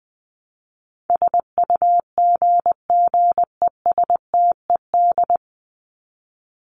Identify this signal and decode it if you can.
Morse code, heard as a keyed tone, about 20 wpm, 700 Hz. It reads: SUGGESTED
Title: ED Suffix - Top 100 Words with Base Words 20wpm